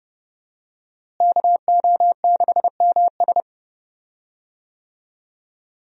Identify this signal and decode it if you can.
Morse code, heard as a keyed tone, about 30 wpm, 700 Hz. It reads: KO6MS
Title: USA Call Signs - 5 Characters - Repeated 2x - 30wpm